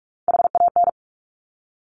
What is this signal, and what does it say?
Morse code, heard as a keyed tone, about 46 wpm, 690 Hz. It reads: HAN
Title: Top Three Letter Combinations in Words 45wpm